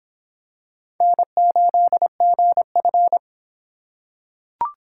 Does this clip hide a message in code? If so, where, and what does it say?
Yes: Morse code from 1.0 to 3.2 s, heard as a keyed tone, about 26 wpm, 700 Hz. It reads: N8GF